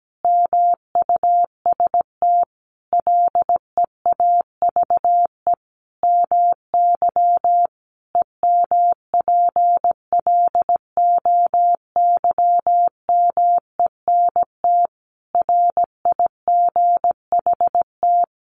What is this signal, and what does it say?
Morse code, heard as a keyed tone, about 17 wpm, 700 Hz. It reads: MUST LEAVE MY EMPLOYMENT RIGHT
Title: Sentences from Top 1500 Words 17wpm